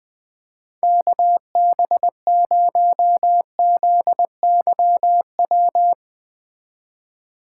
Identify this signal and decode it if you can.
Morse code, heard as a keyed tone, about 20 wpm, 700 Hz. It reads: KB0ZYW